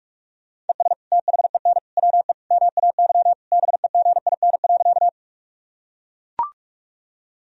Morse code, heard as a keyed tone, about 42 wpm, 700 Hz. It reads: ES THEN WE MAY BEGIN.
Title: Sentences from Top 300 Words and Abbreviations 45wpm